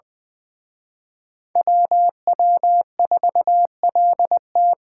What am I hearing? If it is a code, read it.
Morse code, heard as a keyed tone, about 24 wpm, 700 Hz. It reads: WW4LT
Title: Speed Racing - USA 2x2 Call Signs 20wpm